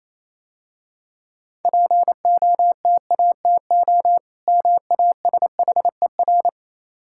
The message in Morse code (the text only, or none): POTATO MASHER